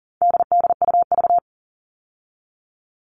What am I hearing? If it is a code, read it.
Morse code, heard as a keyed tone, about 40 wpm, 700 Hz. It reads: DDUV
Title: All Cut Numbers - 4-Digit 40wpm